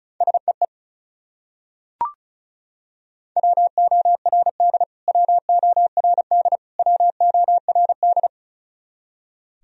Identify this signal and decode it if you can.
Morse code, heard as a keyed tone, about 36 wpm, 700 Hz. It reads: SEE WORD WORD WORD
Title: Top 100 Words Repeated 35wpm